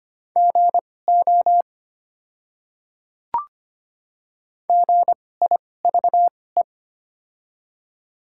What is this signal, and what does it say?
Morse code, heard as a keyed tone, about 25 wpm, spaced as for 18 wpm, 690 Hz. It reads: GO GIVE
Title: 2x Character Spacing - Top 300 Words - 25wpm